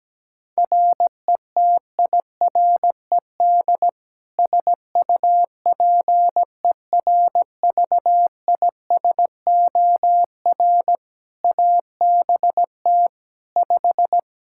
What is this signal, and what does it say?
Morse code, heard as a keyed tone, about 17 wpm, 700 Hz. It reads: RETIRED SUPERVISOR ABT 5
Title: Practice QSO - Exchange 3 - More Info - 17wpm